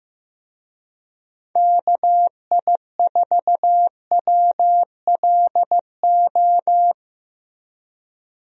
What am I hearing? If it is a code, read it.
Morse code, heard as a keyed tone, about 15 wpm, 700 Hz. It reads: KI4WLO
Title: USA Call Signs - 6 Characters - Repeated 2x - 15wpm